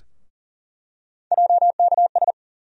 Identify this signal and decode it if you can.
Morse code, heard as a keyed tone, about 40 wpm, 700 Hz. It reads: JKS
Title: USA Call Sign Postfixes - 3 Letters 40wpm